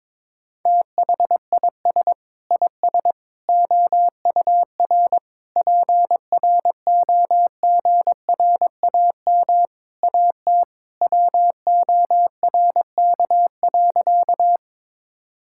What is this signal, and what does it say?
Morse code, heard as a keyed tone, about 21 wpm, 700 Hz. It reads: THIS IS OUR PROGRAM AT WORK.
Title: Sentences from Top 1000 Words - Review Part 5 - 22wpm